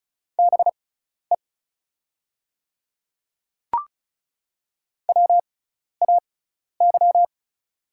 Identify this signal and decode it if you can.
Morse code, heard as a keyed tone, about 34 wpm, spaced as for 12 wpm, 700 Hz. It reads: BE WAY
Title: 6x Character Spacing - Top 200 Words - 35wpm